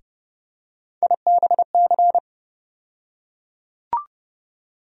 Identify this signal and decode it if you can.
Morse code, heard as a keyed tone, about 30 wpm, 700 Hz. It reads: IBC
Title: USA Call Sign Postfixes - 3 Letters 30wpm